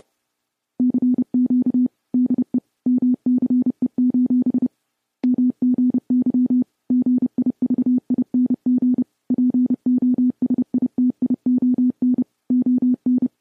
Morse code, heard as a keyed tone, about 30 wpm, 250 Hz. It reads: CQ DE MCE: MGY GIVING POSITION ON